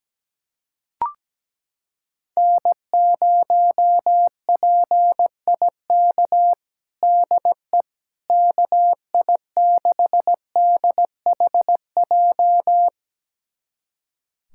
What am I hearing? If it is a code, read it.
Morse code, heard as a keyed tone, about 17 wpm, 700 Hz. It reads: N0PIK DE KI6DHJ
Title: USA Callsign DE Callsign 17wpm